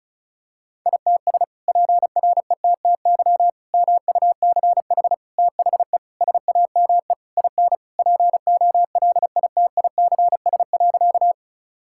Morse code, heard as a keyed tone, about 36 wpm, 700 Hz. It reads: ITS PRETTY MUCH THE SAME IN POLITICS.
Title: Sentences from Top 1000 Words - Review Part 6 - 35wpm